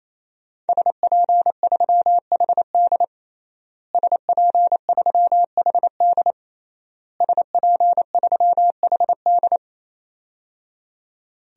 Morse code, heard as a keyed tone, about 28 wpm, 700 Hz. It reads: SP3HD SP3HD SP3HD
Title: DXCC-100 Call Signs - 5 Characters - Repeated 3x - 28wpm